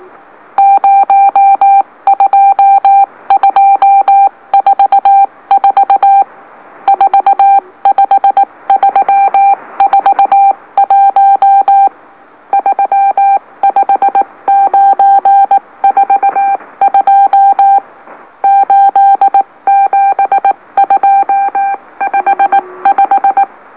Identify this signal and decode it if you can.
Morse code, heard as a keyed tone, about 18 wpm, 790 Hz. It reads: 02244 45341 35942 87255
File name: M04- -2345utc-LOLO